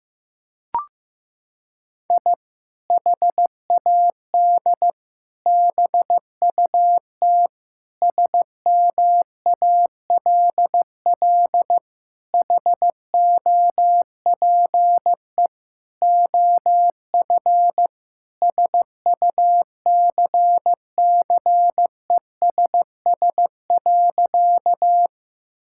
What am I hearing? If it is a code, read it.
Morse code, heard as a keyed tone, about 15 wpm, 700 Hz. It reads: I HAD BUT SMALL HOPE OF SUCCESS.
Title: Sentences from Top 1000 Words - Review Part 6 - 15wpm